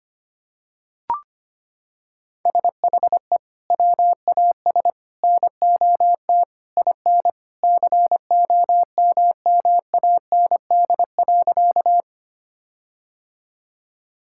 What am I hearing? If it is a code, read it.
Morse code, heard as a keyed tone, about 25 wpm, 700 Hz. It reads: SHE WAS NOT IN COMMAND.
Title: Sentences from Top 1600 Words 25wpm